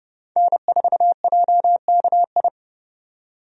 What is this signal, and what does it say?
Morse code, heard as a keyed tone, about 30 wpm, 700 Hz. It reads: N4JKI
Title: USA Call Signs - 5 Characters - 30wpm